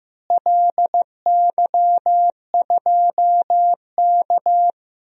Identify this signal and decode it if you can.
Morse code, heard as a keyed tone, about 16 wpm, 700 Hz. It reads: LY2K
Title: DXCC-100 Call Signs - 4 Characters - Repeated 3x - 15wpm